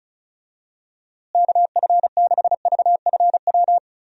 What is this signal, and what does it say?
Morse code, heard as a keyed tone, about 34 wpm, 700 Hz. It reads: KF6VFW
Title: USA Call Signs - 6 Characters - Repeated 3x - 35wpm